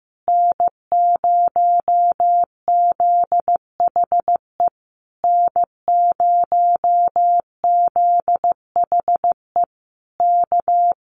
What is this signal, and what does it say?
Morse code, heard as a keyed tone, about 15 wpm, 700 Hz. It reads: N0ZHE N0ZHE K
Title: Common QSO Phrases Encore 15wpm